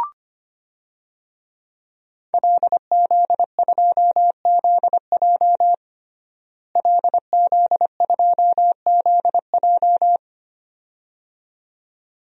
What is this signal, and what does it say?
Morse code, heard as a keyed tone, about 25 wpm, 700 Hz. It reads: LZ2ZJ LZ2ZJ
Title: DXCC-100 Call Signs - 5 Characters - Repeated 2x - 25wpm